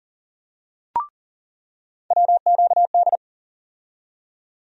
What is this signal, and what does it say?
Morse code, heard as a keyed tone, about 40 wpm, 700 Hz. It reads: WQD
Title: USA Call Sign Postfixes - 3 Letters 40wpm